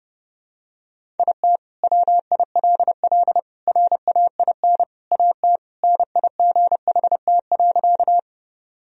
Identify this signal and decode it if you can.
Morse code, heard as a keyed tone, about 30 wpm, 700 Hz. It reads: IT WILL RAIN AT NIGHT.
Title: Sentences from Top 1000 Words - Review Part 1 - 30wpm